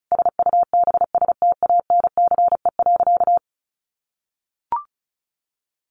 Morse code, heard as a keyed tone, about 34 wpm, 700 Hz. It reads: SUBSTANCE.
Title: Sentences from Top 1500 Words 35wpm